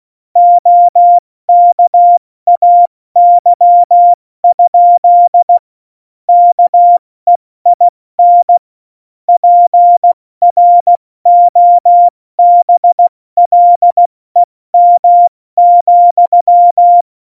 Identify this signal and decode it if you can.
Morse code, heard as a keyed tone, about 16 wpm, 700 Hz. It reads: OKAY? KEIN PROBLEM,
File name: Morsekurs / Witze / Witz 02 Juni 2020, 70 F80